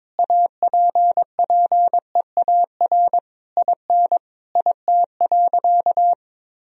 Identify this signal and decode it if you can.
Morse code, heard as a keyed tone, about 22 wpm, 700 Hz. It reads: APPEAR IN IT.
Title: Sentences from Top 1000 Words - Review Part 2 - 22wpm